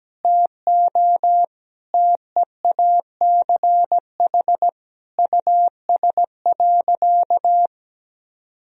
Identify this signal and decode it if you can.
Morse code, heard as a keyed tone, about 17 wpm, 700 Hz. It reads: TO TEACH US.
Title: Sentences from Top 1000 Words - Review Part 2 - 17wpm